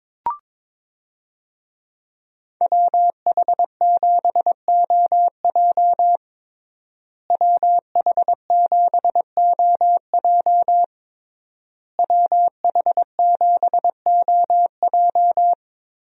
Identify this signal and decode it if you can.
Morse code, heard as a keyed tone, about 22 wpm, 700 Hz. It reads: WH7OJ WH7OJ WH7OJ
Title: USA Call Signs - 5 Characters - Repeated 3x - 22wpm